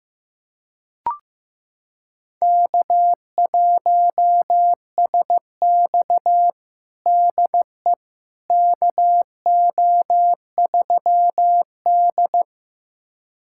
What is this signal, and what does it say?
Morse code, heard as a keyed tone, about 15 wpm, 700 Hz. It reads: K1SX DE KO3D
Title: USA Callsign DE Callsign - 4 Characters - 15wpm